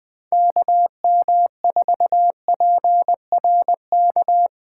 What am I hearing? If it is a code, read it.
Morse code, heard as a keyed tone, about 20 wpm, 700 Hz. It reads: KM4PRK
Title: Speed Racing - USA Call Signs 17wpm